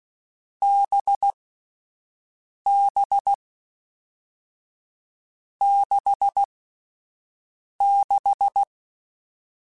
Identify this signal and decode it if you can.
Morse code, heard as a keyed tone, about 16 wpm, 800 Hz. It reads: B B 6 6
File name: Short Runs / Short Runs-mp3s / NDB NDB6 @10